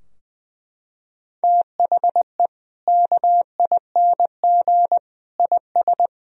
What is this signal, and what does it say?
Morse code, heard as a keyed tone, about 20 wpm, 700 Hz. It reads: THE KING IS
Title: Sentences from Top 1000 Words - Review Part 4 - 20wpm